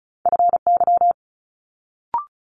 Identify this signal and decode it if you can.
Morse code, heard as a keyed tone, about 36 wpm, 700 Hz. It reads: FY
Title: Multi-letter Phonemes Quick Fire 35wpm